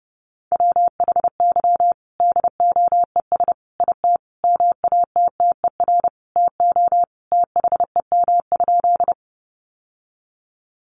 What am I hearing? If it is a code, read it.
Morse code, heard as a keyed tone, about 29 wpm, 700 Hz. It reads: WHY DOES IT MATTER TO THEM?
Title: Sentences from Top 700 Words 30wpm